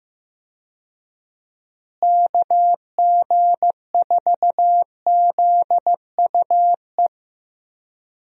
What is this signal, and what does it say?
Morse code, heard as a keyed tone, about 15 wpm, 700 Hz. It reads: KG4ZUE
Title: USA Call Signs - 6 Characters - Repeated 2x - 15wpm